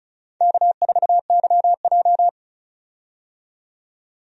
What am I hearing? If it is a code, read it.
Morse code, heard as a keyed tone, about 34 wpm, 700 Hz. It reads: K4YJ
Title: USA Call Signs - 4 Characters - 35wpm